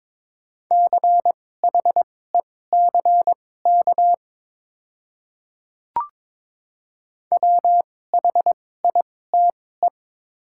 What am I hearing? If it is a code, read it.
Morse code, heard as a keyed tone, about 22 wpm, spaced as for 16 wpm, 700 Hz. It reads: CHECK WHITE
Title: 2x Character Spacing - Top 500 Words - 22wpm